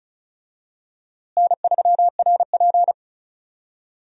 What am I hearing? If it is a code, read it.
Morse code, heard as a keyed tone, about 36 wpm, 700 Hz. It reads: N3RP
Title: USA Call Signs - 4 Characters - Repeated 2x - 35wpm